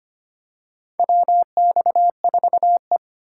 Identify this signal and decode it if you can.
Morse code, heard as a keyed tone, about 25 wpm, 700 Hz. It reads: WX4E
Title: USA Call Signs - 4 Characters - 2x Character Spacing - 25wpm